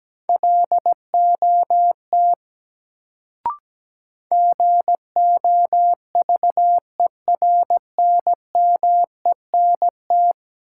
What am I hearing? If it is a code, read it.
Morse code, heard as a keyed tone, about 17 wpm, 700 Hz. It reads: LOT GOVERNMENT
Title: Sets of 2 Words - New Word Drill - 17wpm